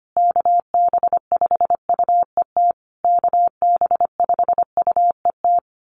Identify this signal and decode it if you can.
Morse code, heard as a keyed tone, about 25 wpm, 700 Hz. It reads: KB5UET KB5UET
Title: USA Call Signs Repeated 3x 25wpm